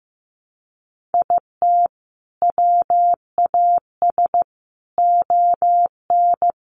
Morse code, heard as a keyed tone, about 15 wpm, 700 Hz. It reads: IT WAS ON
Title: Sentences from Top 800 Words 15wpm